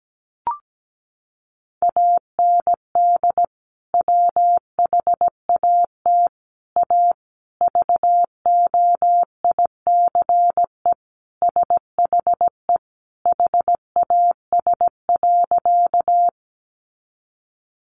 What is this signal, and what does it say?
Morse code, heard as a keyed tone, about 17 wpm, 700 Hz. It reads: AND WHAT A VOICE SHE HAS.
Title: Sentences from Top 500 Words 17wpm